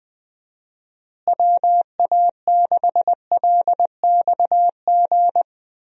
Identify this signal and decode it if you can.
Morse code, heard as a keyed tone, about 20 wpm, 700 Hz. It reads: WA6LXG
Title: USA Call Signs - 6 Characters - Repeated 3x - 20wpm